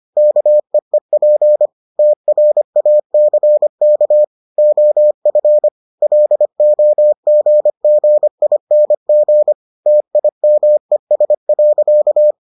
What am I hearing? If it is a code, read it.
Morse code, heard as a keyed tone, about 25 wpm, 600 Hz. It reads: KEEP TRACK OF LOGGING TIMES.